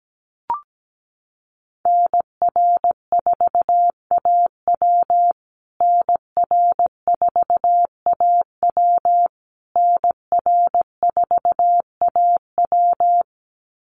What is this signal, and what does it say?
Morse code, heard as a keyed tone, about 17 wpm, 700 Hz. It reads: NR4AW NR4AW NR4AW
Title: USA Call Signs Repeated 3x 17wpm